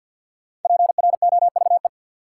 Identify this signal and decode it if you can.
Morse code, heard as a keyed tone, about 53 wpm, 700 Hz. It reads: PROVE
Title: Sentences from Top 1600 Words - New Word Drill Alt - 50wpm